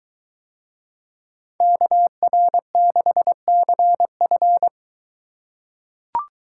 Morse code, heard as a keyed tone, about 23 wpm, 700 Hz. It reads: KR6CF